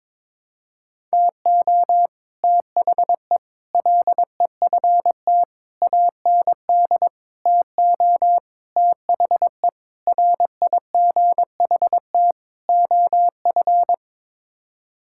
Morse code, heard as a keyed tone, about 22 wpm, 700 Hz. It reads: TO THE LEFT AND TO THE RIGHT OF
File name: practice / 22wpm / Sets of 8 Words - 22wpm